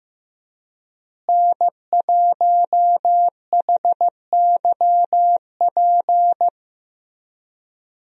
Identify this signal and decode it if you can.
Morse code, heard as a keyed tone, about 15 wpm, 700 Hz. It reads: N1HYP
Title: USA Call Signs - 5 Characters - Repeated 2x - 15wpm